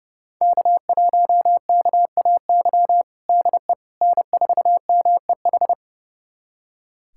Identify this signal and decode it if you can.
Morse code, heard as a keyed tone, about 30 wpm, 700 Hz. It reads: K1KAY DE N4MEH
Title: USA Callsign DE Callsign - 5 Characters - 30wpm